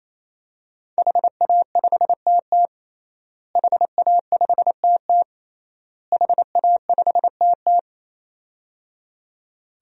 Morse code, heard as a keyed tone, about 28 wpm, 700 Hz. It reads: HA5TT HA5TT HA5TT
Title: DXCC-100 Call Signs - 5 Characters - Repeated 3x - 28wpm